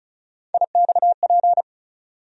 Morse code, heard as a keyed tone, about 36 wpm, 700 Hz. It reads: IXP
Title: USA Call Sign Postfixes - 3 Letters 35wpm